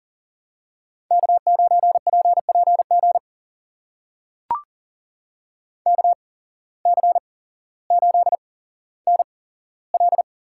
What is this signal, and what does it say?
Morse code, heard as a keyed tone, about 40 wpm, spaced as for 17 wpm, 700 Hz. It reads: K9PPG KC8NL
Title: USA Call Signs - 8x Character Spacing - 40wpm